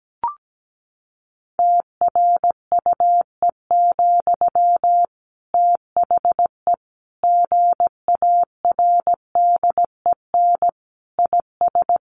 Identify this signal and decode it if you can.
Morse code, heard as a keyed tone, about 17 wpm, 700 Hz. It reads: TRUE, THE GARDEN IS
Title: Sentences from Top 700 Words 17wpm